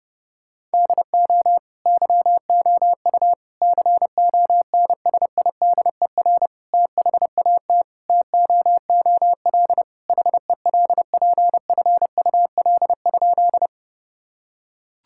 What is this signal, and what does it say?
Morse code, heard as a keyed tone, about 32 wpm, 700 Hz. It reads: DO YOU CONSIDER THAT TOOL HELPFUL?
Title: Sentences from Top 1000 Words - Review Part 6 - 30wpm